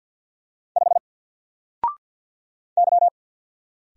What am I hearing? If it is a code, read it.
Morse code, heard as a keyed tone, about 50 wpm, 700 Hz. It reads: 5 Y
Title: Single Letter-Number 50wpm